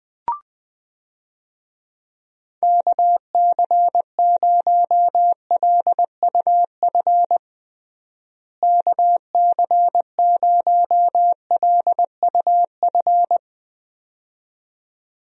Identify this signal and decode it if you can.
Morse code, heard as a keyed tone, about 20 wpm, 700 Hz. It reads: KC0LUF KC0LUF
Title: USA Call Signs - 6 Characters - Repeated 2x - 20wpm